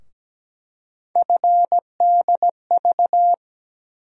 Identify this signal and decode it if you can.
Morse code, heard as a keyed tone, about 17 wpm, 700 Hz. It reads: FDV
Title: USA Call Sign Postfixes - 3 Letters 17wpm